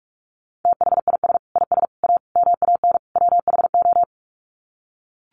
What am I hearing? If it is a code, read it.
Morse code, heard as a keyed tone, about 44 wpm, 700 Hz. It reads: THIS IS A MAN WHO
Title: Sets of 5 Words Encore2 45wpm